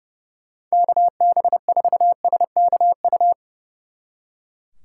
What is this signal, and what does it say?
Morse code, heard as a keyed tone, about 37 wpm, 700 Hz. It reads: KB4SKU
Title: Speed Racing - USA 2x3 Call Signs 30wpm